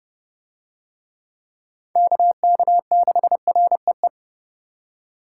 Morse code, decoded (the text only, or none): KK6REE